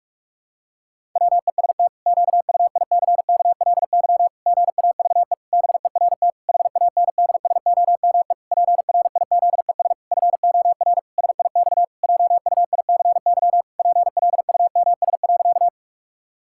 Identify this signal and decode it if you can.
Morse code, heard as a keyed tone, about 46 wpm, 700 Hz. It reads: WEST QUICKLY GAVE BERT HANDSOME PRIZES FOR SIX JUICY PLUMS.
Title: Pangram Challenge v2 - 45wpm